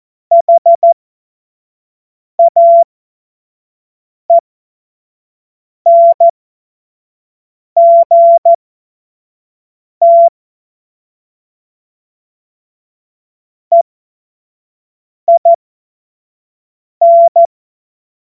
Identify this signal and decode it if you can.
Morse code, heard as a keyed tone, about 14 wpm, spaced as for 5 wpm, 680 Hz. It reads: HAENGT EIN
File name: Witz-21-Oktober-2025-25-F70